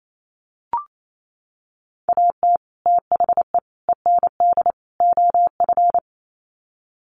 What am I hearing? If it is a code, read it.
Morse code, heard as a keyed tone, about 28 wpm, 700 Hz. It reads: AT THE END OF
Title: Sets of 4 Words 28wpm